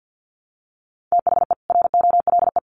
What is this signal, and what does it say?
Morse code, heard as a keyed tone, about 48 wpm, 700 Hz. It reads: THE ROLE
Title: Sets of 3 Words 50wpm